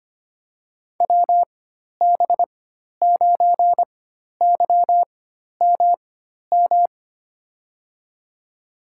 Morse code, heard as a keyed tone, about 25 wpm, spaced as for 12 wpm, 700 Hz. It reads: WB9YMM
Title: USA Call Signs - 6 Characters - 4x Character Spacing - 25wpm